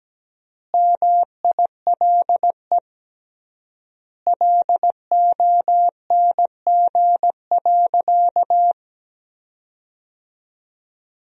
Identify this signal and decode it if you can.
Morse code, heard as a keyed tone, about 17 wpm, 700 Hz. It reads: MILE LONG.
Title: 3x Word-Spacing - Sentences from Top 500 Words - 17wpm